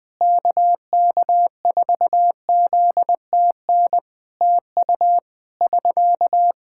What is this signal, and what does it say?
Morse code, heard as a keyed tone, about 20 wpm, 700 Hz. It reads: KK4ZTN TU <SK>
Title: Practice QSO - Exchange 4 - Ending - 20wpm